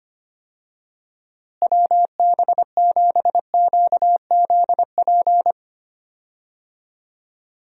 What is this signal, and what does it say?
Morse code, heard as a keyed tone, about 25 wpm, 700 Hz. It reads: WB7QZP
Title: USA Call Signs - 6 Characters - Repeated 2x - 25wpm